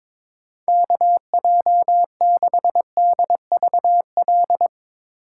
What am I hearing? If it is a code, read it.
Morse code, heard as a keyed tone, about 22 wpm, 700 Hz. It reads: KJ6DVL
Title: Speed Racing - USA Call Signs 22wpm